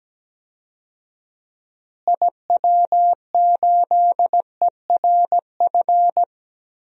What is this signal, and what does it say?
Morse code, heard as a keyed tone, about 17 wpm, 700 Hz. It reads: IW8ERF
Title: DXCC-100 Call Signs - 6 Characters - Repeated 3x - 17wpm